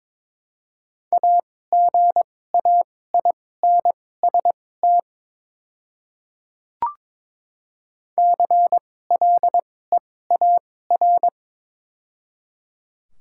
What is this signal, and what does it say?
Morse code, heard as a keyed tone, about 22 wpm, spaced as for 16 wpm, 700 Hz. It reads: AGAINST CLEAR
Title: 2x Character Spacing - Top 500 Words - 22wpm